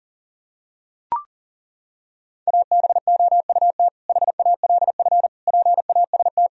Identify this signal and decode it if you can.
Morse code, heard as a keyed tone, about 40 wpm, 700 Hz. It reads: ABOUT HALF PAST
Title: Sentences from Top 1000 Words - Review Part 3 - 40wpm